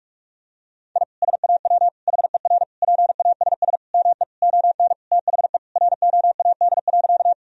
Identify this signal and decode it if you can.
Morse code, heard as a keyed tone, about 42 wpm, 700 Hz. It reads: I SAW HER PASS ME ON THE ROAD.
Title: Sentences from Top 1000 Words - Review Part 2 - 45wpm